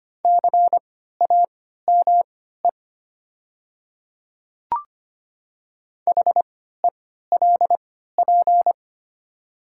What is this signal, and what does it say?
Morse code, heard as a keyed tone, about 25 wpm, spaced as for 14 wpm, 700 Hz. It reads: CAME HELP
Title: 3x Character Spacing - Top 300 Words - 25wpm